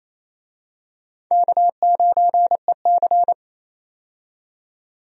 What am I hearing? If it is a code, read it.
Morse code, heard as a keyed tone, about 28 wpm, 700 Hz. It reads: K9EC
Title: USA Call Signs - 4 Characters - 28wpm